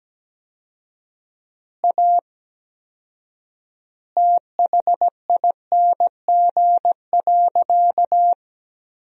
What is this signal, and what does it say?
Morse code, heard as a keyed tone, about 17 wpm, 700 Hz. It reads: A THING.
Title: 4x Word-Spacing - Sentences from Top 200 Words - 17wpm